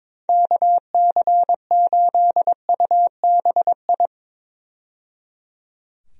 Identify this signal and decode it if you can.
Morse code, heard as a keyed tone, about 21 wpm, 700 Hz. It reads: KC8UBI
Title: USA Call Signs - 6 Characters - 22wpm